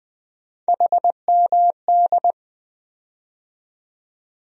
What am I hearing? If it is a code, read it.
Morse code, heard as a keyed tone, about 20 wpm, 700 Hz. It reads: HMD
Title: USA Call Sign Postfixes - 3 Letters 20wpm